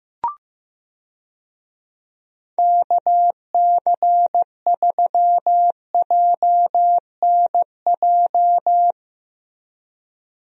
Morse code, heard as a keyed tone, about 15 wpm, 700 Hz. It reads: KC3JNJ
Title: USA Call Signs - 6 Characters - Repeated 2x - 15wpm